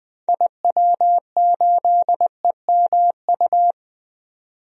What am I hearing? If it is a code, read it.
Morse code, heard as a keyed tone, about 20 wpm, 700 Hz. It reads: IW8EMU
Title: DXCC-100 Call Signs - 6 Characters - 20wpm